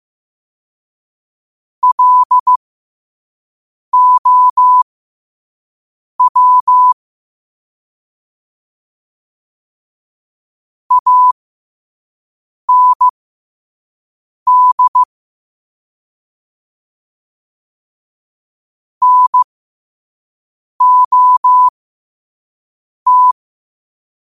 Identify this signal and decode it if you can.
Morse code, heard as a keyed tone, about 15 wpm, spaced as for 5 wpm, 1000 Hz. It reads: LOW AND NOT
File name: qotdfiles / 5wpm